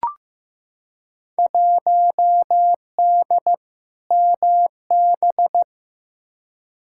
Morse code, heard as a keyed tone, about 15 wpm, 700 Hz. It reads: 1D MB
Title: Field Day Exchanges Encore - 15wpm